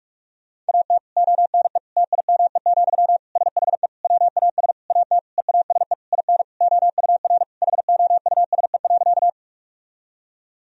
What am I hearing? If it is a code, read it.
Morse code, heard as a keyed tone, about 42 wpm, 700 Hz. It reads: AT ONE TIME, SHE WAS AT EASE IN OUR HOUSE.
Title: Sentences from Top 1000 Words - Review Part 1 - 45wpm